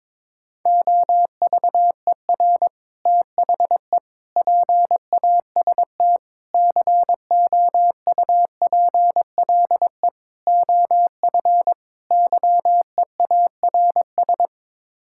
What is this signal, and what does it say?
Morse code, heard as a keyed tone, about 22 wpm, 700 Hz. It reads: OVER THE PAST COUPLE OF YEARS
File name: practice / 22wpm / Sets of 6 Words - 22wpm